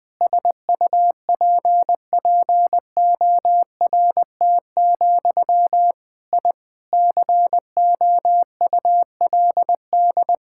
Morse code, heard as a keyed tone, about 20 wpm, 700 Hz. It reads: SUPPORT, I COULD
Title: Sentences from Top 1000 Words - Review Part 5 - 20wpm